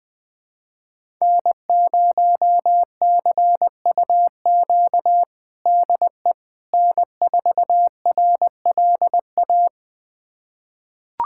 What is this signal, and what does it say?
Morse code, heard as a keyed tone, about 20 wpm, 700 Hz. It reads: N0CUQ DE N4RLA
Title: USA Callsign DE Callsign - 5 Characters - 20wpm